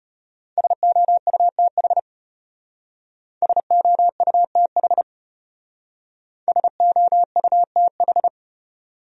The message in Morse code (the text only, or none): SOUTH SOUTH SOUTH